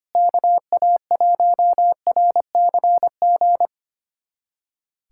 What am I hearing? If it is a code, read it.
Morse code, heard as a keyed tone, about 25 wpm, 700 Hz. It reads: KA1RCG
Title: USA Callsign DE Callsign 25wpm